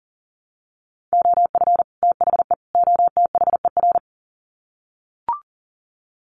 Morse code, heard as a keyed tone, about 40 wpm, 700 Hz. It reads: OF THE OTHER
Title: Sets of 3 Words 40wpm